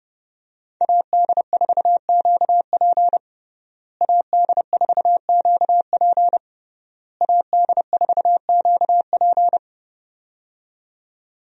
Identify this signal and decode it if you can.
Morse code, heard as a keyed tone, about 32 wpm, 700 Hz. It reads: AD4QP AD4QP AD4QP
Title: USA Call Signs - 5 Characters - Repeated 3x - 30wpm